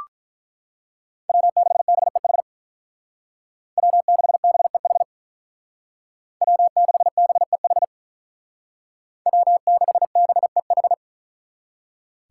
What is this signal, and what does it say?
Morse code, heard as a keyed tone, about 43 wpm, 700 Hz. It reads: W6BEH W6BEH W6BEH W6BEH
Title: Speed Racing - USA 1x3 Call Signs 35wpm